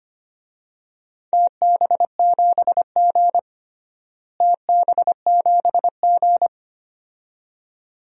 Morse code, heard as a keyed tone, about 25 wpm, 700 Hz. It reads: TB7G TB7G
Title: DXCC-100 Call Signs - 4 Characters - Repeated 2x - 25wpm